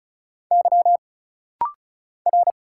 Morse code, heard as a keyed tone, about 34 wpm, 700 Hz. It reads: Y R
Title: Single Letters - Rapid-Fire 35wpm